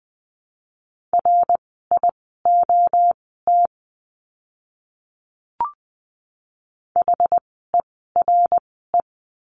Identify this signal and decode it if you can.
Morse code, heard as a keyed tone, about 20 wpm, spaced as for 14 wpm, 700 Hz. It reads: RIOT HERE
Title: 061 Lesson 11f - Letter h - Words 20wpm